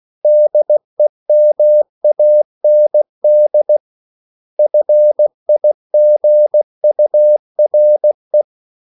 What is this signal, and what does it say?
Morse code, heard as a keyed tone, about 16 wpm, 600 Hz. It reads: DEMAND FIGURE
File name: CW / Practice / 500MostCommonEnglishWords16